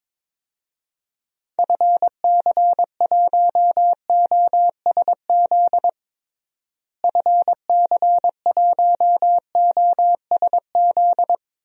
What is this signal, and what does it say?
Morse code, heard as a keyed tone, about 22 wpm, 700 Hz. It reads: FC1OSZ FC1OSZ
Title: DXCC-100 Call Signs - 6 Characters - Repeated 3x - 22wpm